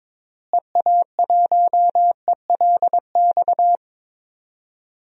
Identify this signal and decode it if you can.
Morse code, heard as a keyed tone, about 22 wpm, 700 Hz. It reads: EA1ELX
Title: DXCC-100 Call Signs - 6 Characters - Repeated 2x - 22wpm